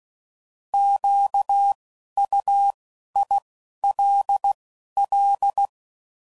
Morse code, heard as a keyed tone, about 16 wpm, 800 Hz. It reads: Q U I L L
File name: Prefixes / By First Letter / Qs